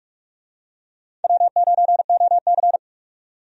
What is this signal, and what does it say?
Morse code, heard as a keyed tone, about 43 wpm, 700 Hz. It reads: W9OC
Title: USA Call Signs - 4 Characters - 45wpm